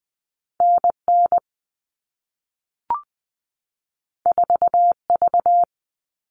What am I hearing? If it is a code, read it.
Morse code, heard as a keyed tone, about 20 wpm, 700 Hz. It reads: NN 4V
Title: All Cut Numbers - 2-Digit 20wpm